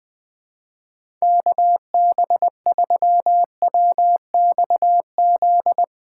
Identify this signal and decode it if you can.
Morse code, heard as a keyed tone, about 21 wpm, 700 Hz. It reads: KB3WXZ
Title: USA Callsign DE Callsign - 6 Characters - 20wpm